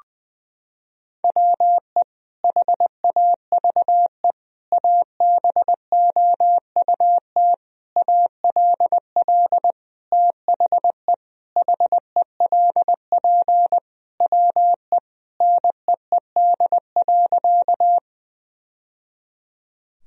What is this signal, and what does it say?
Morse code, heard as a keyed tone, about 20 wpm, 700 Hz. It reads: WE HAVE ABOUT ALL THE HELP WE NEED.
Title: Sentences from Top 1000 Words - Review Part 1 - 20wpm